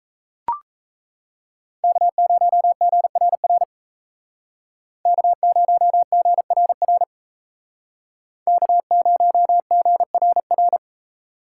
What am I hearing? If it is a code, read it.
Morse code, heard as a keyed tone, about 36 wpm, 700 Hz. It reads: K0GRR K0GRR K0GRR